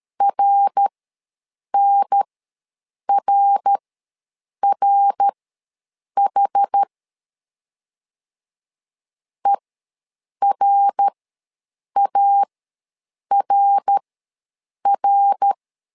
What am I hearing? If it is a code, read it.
Morse code, heard as a keyed tone, about 13 wpm, spaced as for 7 wpm, 800 Hz. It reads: RNRRH ERARR
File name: LEZIONI / 3 MWRO / 3 MWR ws 12wpm 1809